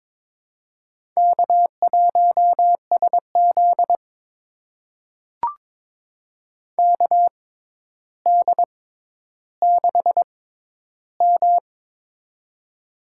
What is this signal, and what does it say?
Morse code, heard as a keyed tone, about 22 wpm, spaced as for 11 wpm, 700 Hz. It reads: K1SZ KD6M
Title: USA Call Signs - 4 Characters - 6x Character Spacing - 22wpm